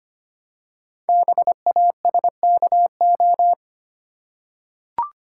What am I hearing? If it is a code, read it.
Morse code, heard as a keyed tone, about 25 wpm, 700 Hz. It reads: BASKO
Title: Code Groups - 5 Letters - 25wpm